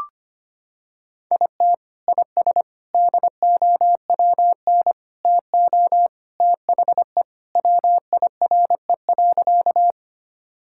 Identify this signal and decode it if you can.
Morse code, heard as a keyed tone, about 25 wpm, 700 Hz. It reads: IT IS DOWN TO THE WIRE.
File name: practice / 25wpm / Sentences with 3-Word Phrases - 25wpm